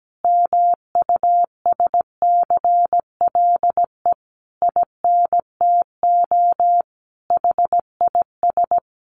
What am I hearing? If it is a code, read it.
Morse code, heard as a keyed tone, about 17 wpm, 700 Hz. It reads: MUSCLE INTO HIS
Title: Sentences from Top 1300 Words 17wpm